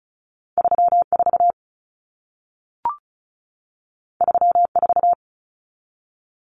Morse code, heard as a keyed tone, about 36 wpm, 700 Hz. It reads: 34 34
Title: Common Cut Numbers - 2-Digit 35wpm